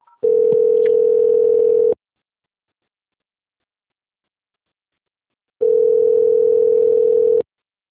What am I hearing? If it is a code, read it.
Ringback tone (North America), 440 + 480 Hz.